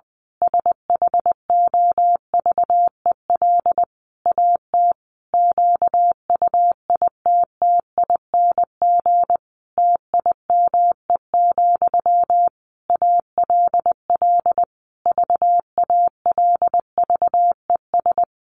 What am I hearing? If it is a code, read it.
Morse code, heard as a keyed tone, about 20 wpm, 700 Hz. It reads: SHOVEL AT QUITTING TIME, ALL VALVES
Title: A Sound Of Thunder 20wpm